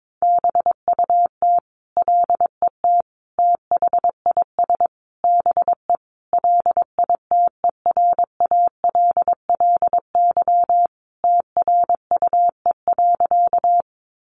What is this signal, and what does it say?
Morse code, heard as a keyed tone, about 22 wpm, 700 Hz. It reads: BUT LET THIS BE LITERALLY TRUE.
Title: Sentences from Top 1400 Words 22wpm